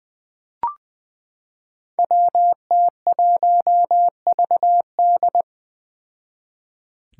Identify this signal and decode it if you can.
Morse code, heard as a keyed tone, about 20 wpm, 700 Hz. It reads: WT1VD
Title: Five Challenging Letter-Number Combinations 20wpm